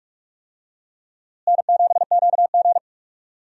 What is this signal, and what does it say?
Morse code, heard as a keyed tone, about 47 wpm, 700 Hz. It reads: N7QG
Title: USA Call Signs - 4 Characters - Repeated 2x - 45wpm